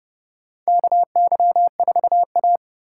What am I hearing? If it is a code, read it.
Morse code, heard as a keyed tone, about 30 wpm, 700 Hz. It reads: KY4A
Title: USA Call Signs - 4 Characters - 3x Character Spacing - 30wpm